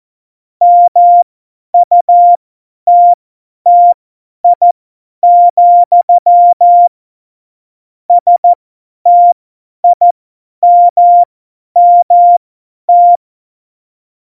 Morse code, heard as a keyed tone, about 14 wpm, spaced as for 10 wpm, 700 Hz. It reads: MUTTI, STIMMT
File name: Morsekurs / Witze / Witz 26. September 17, Tempo 50